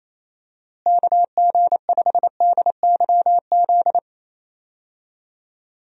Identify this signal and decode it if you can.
Morse code, heard as a keyed tone, about 28 wpm, 700 Hz. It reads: KG5DYZ
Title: USA Call Signs - 6 Characters - Repeated 2x - 28wpm